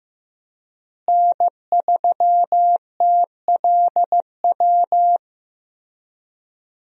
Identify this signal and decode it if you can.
Morse code, heard as a keyed tone, about 15 wpm, 700 Hz. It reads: N3TLW
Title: USA Call Signs - 5 Characters - Repeated 3x - 15wpm